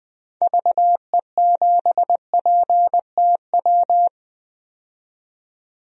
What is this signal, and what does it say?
Morse code, heard as a keyed tone, about 20 wpm, 700 Hz. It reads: VE7PTW
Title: DXCC-100 Call Signs - 6 Characters - 20wpm